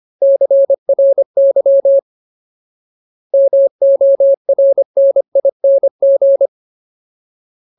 Morse code, heard as a keyed tone, about 25 wpm, 550 Hz. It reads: CRY MORNING
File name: Ultimate Word List / 25WPM Ultimate